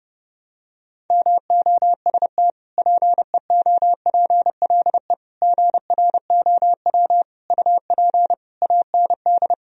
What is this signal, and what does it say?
Morse code, heard as a keyed tone, about 30 wpm, 700 Hz. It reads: MOST PEOPLE GROW UP AND
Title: Sentences from Top 1000 Words - Review Part 1 - 30wpm